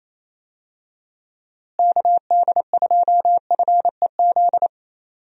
Morse code, heard as a keyed tone, about 28 wpm, 700 Hz. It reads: KD2FEZ
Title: USA Call Signs - 6 Characters - Repeated 2x - 28wpm